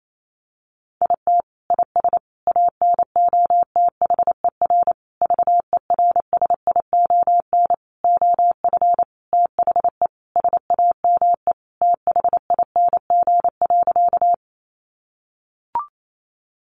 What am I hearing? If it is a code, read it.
Morse code, heard as a keyed tone, about 28 wpm, 700 Hz. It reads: IT IS ANOTHER VERSION OF THE SAME THING.
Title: Sentences from Top 1400 Words 28wpm